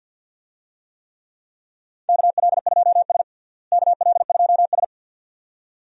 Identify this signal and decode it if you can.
Morse code, heard as a keyed tone, about 52 wpm, 700 Hz. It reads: KL2S KL2S
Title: USA Call Signs - 4 Characters - Repeated 2x - 50wpm